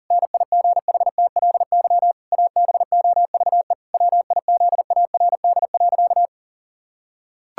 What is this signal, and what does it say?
Morse code, heard as a keyed tone, about 40 wpm, 700 Hz. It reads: NIGHTLY ABOVE WIZARD.
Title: Pangram Challenge v2 - 40wpm